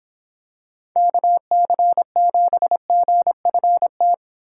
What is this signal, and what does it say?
Morse code, heard as a keyed tone, about 26 wpm, 700 Hz. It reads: KC7GFT